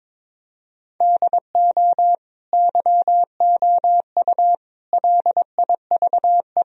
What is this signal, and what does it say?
Morse code, heard as a keyed tone, about 22 wpm, 700 Hz. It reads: DO YOU LIVE
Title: Sentences from Top 1000 Words - Review Part 1 - 22wpm